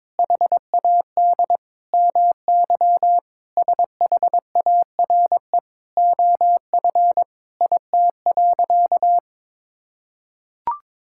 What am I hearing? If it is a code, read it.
Morse code, heard as a keyed tone, about 22 wpm, 700 Hz. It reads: HAD MY SHARE OF IT.
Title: Sentences from Top 1000 Words - Review Part 6 - 22wpm